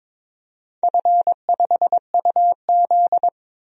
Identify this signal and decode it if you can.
Morse code, heard as a keyed tone, about 22 wpm, 700 Hz. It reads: F5UZ
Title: DXCC-100 Call Signs - 4 Characters - 22wpm